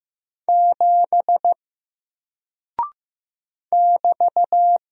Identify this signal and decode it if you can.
Morse code, heard as a keyed tone, about 15 wpm, 700 Hz. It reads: 7 <BT>
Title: Single Character (Letter:Number:Punctuation:Prosign) 15wpm